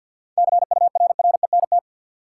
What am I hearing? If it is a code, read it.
Morse code, heard as a keyed tone, about 47 wpm, 700 Hz. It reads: CURRENT
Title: Sentences from Top 1000 Words - Review Part 4 - New Word Drill - 50wpm